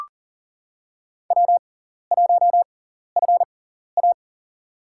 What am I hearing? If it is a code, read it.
Morse code, heard as a keyed tone, about 40 wpm, spaced as for 14 wpm, 700 Hz. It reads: W1FA
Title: USA Call Signs - 4 Characters - 6x Character Spacing - 40wpm